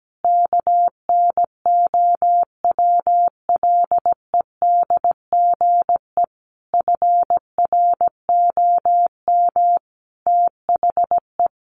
Morse code, heard as a keyed tone, about 17 wpm, 700 Hz. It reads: KNOWLEDGE FROM THE
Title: Sentences from Top 1300 Words 17wpm